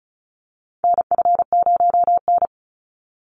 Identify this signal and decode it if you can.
Morse code, heard as a keyed tone, about 34 wpm, 700 Hz. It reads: NF0N
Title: Call Signs Repeated 3x - 35wpm